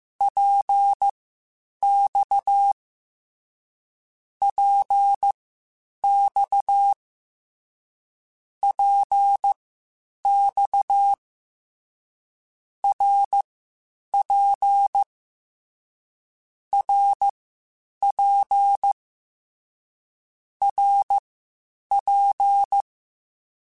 Morse code, heard as a keyed tone, about 15 wpm, spaced as for 8 wpm, 800 Hz. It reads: PX PX PX RP RP RP
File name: Short Runs / Short Runs-mp3s / Balanced @13